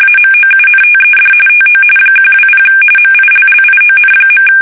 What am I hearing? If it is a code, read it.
RTTY, about 45.45 baud. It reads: AMOUNTED TO 120 MILLION.
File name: fsk850 / USS-Massachusetts